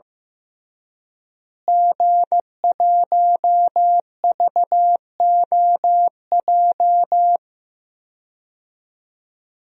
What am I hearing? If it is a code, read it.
Morse code, heard as a keyed tone, about 15 wpm, 700 Hz. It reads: G1VOJ
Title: DXCC-100 Call Signs - 5 Characters - Repeated 3x - 15wpm